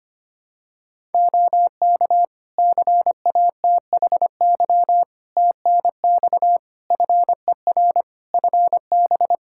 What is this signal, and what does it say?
Morse code, heard as a keyed tone, about 25 wpm, 700 Hz. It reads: OK CATHY TNX FER FB
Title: K1USN SST QSO Practice 25wpm